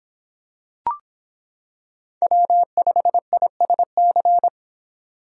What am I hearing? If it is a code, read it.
Morse code, heard as a keyed tone, about 25 wpm, 700 Hz. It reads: W5ISC